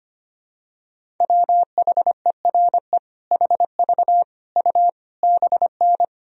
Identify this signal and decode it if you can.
Morse code, heard as a keyed tone, about 25 wpm, 700 Hz. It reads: WHERE HV U BN
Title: Sentences from Top 300 Words and Abbreviations 25wpm